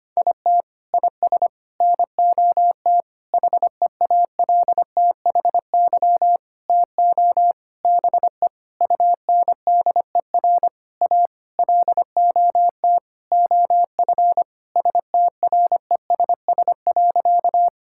Morse code, heard as a keyed tone, about 25 wpm, 700 Hz. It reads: IT IS NOT HEALTHY TO BE UNDER A LOT OF STRESS.
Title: Sentences from Top 1000 Words - Review Part 6 - 25wpm